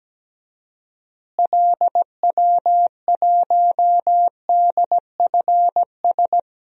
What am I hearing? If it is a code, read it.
Morse code, heard as a keyed tone, about 17 wpm, 700 Hz. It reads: LW1DFS
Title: DXCC-100 Call Signs - 6 Characters - Repeated 2x - 17wpm